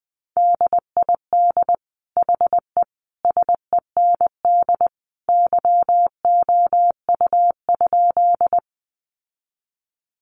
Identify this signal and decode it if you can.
Morse code, heard as a keyed tone, about 20 wpm, 700 Hz. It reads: DID HE SEND YOU?
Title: Sentences from Top 1500 Words 20wpm